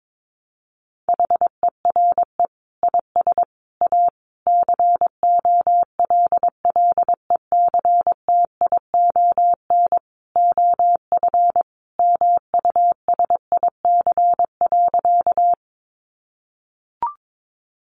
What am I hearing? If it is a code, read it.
Morse code, heard as a keyed tone, about 22 wpm, 700 Hz. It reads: HERE IS A COLLECTION OF MUSIC.
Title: Sentences from Top 1200 Words 22wpm